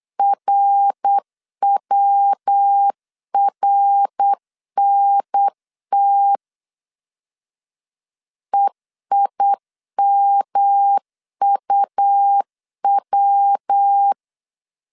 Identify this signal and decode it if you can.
Morse code, heard as a keyed tone, about 8 wpm, 800 Hz. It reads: RWRNT EIMUW
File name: CW / DET / 1 3 1500C S B 8wpm 4141